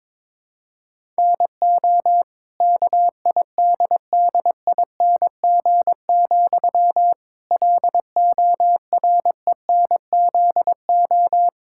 Morse code, heard as a keyed tone, about 22 wpm, 700 Hz. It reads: NO KIDDING, LORENZO
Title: Pangram Challenge v2 - 22wpm